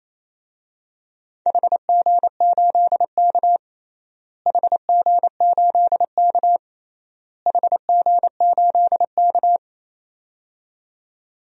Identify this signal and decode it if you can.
Morse code, heard as a keyed tone, about 27 wpm, 700 Hz. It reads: HG8K HG8K HG8K
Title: DXCC-100 Call Signs - 4 Characters - Repeated 3x - 28wpm